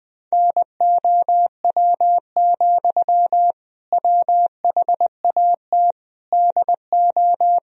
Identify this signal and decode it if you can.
Morse code, heard as a keyed tone, about 20 wpm, 700 Hz. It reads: NOW, WHAT DO
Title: Sentences from Top 200 Words and Abbreviations 20wpm